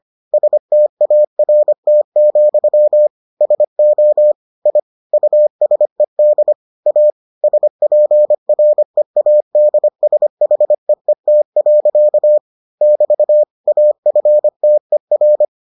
Morse code, heard as a keyed tone, about 25 wpm, 600 Hz. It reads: START, SO I USED A SPREADSHEET. <BT> AFTER